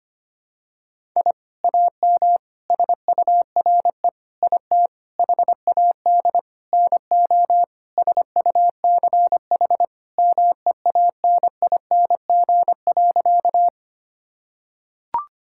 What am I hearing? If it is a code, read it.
Morse code, heard as a keyed tone, about 25 wpm, 700 Hz. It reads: I AM SURE IT HAD NO SUCH MEANING.
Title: Sentences from Top 1000 Words - Review Part 6 - 25wpm